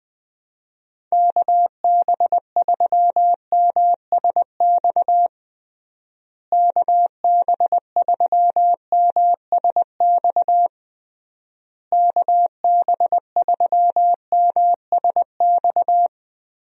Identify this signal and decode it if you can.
Morse code, heard as a keyed tone, about 20 wpm, 700 Hz. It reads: KB3MSX KB3MSX KB3MSX
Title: USA Call Signs - 6 Characters - Repeated 3x - 20wpm